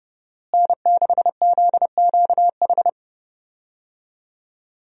Morse code, heard as a keyed tone, about 30 wpm, 700 Hz. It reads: N6ZQH
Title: USA Call Signs - 5 Characters - Repeated 2x - 30wpm